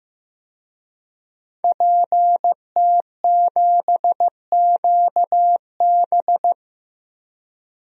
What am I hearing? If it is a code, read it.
Morse code, heard as a keyed tone, about 15 wpm, 700 Hz. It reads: PT7QB
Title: DXCC-100 Call Signs - 5 Characters - Repeated 3x - 15wpm